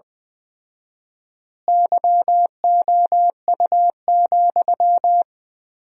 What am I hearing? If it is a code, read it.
Morse code, heard as a keyed tone, about 20 wpm, 700 Hz. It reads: YOU,
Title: 4x Word-Spacing - Sentences from Top 400 Words - 20wpm